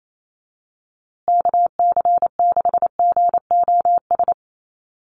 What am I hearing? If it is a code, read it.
Morse code, heard as a keyed tone, about 28 wpm, 700 Hz. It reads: KC6GOS
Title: USA Call Signs Repeated 3x 28wpm